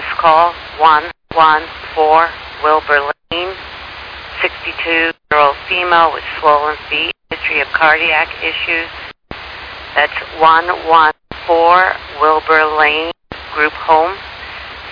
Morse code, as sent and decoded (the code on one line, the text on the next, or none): none